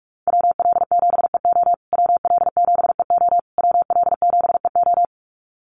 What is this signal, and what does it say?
Morse code, heard as a keyed tone, about 44 wpm, 700 Hz. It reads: WL7EO WL7EO WL7EO
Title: USA Call Signs Repeated 3x 45wpm